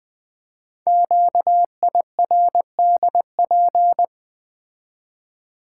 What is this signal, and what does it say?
Morse code, heard as a keyed tone, about 20 wpm, 700 Hz. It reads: QIRDP
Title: Code Groups - 5 Letter:Numbers - 20wpm